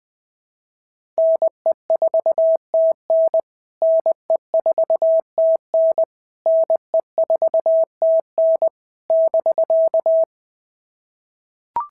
Morse code, heard as a keyed tone, about 20 wpm, 650 Hz. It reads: NE4TN NE4TN NE4TN <BK>
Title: POTA QSO Practice - 20wpm